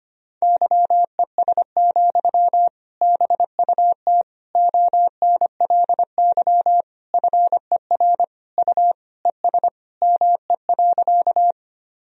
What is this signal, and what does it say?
Morse code, heard as a keyed tone, about 25 wpm, 700 Hz. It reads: YES, BUT ONLY FER U ES ME.
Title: Sentences from Top 500 Words and Abbreviations 25wpm